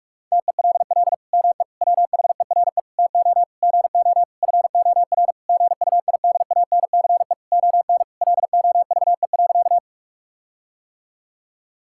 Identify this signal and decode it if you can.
Morse code, heard as a keyed tone, about 46 wpm, 700 Hz. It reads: TELL ME WHERE TO GO FOR GUIDANCE ON LOVE.
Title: Sentences from Top 1600 Words 45wpm